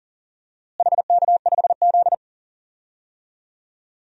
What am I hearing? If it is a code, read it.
Morse code, heard as a keyed tone, about 40 wpm, 700 Hz. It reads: HK5Z
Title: DXCC-100 Call Signs - 4 Characters - Repeated 2x - 40wpm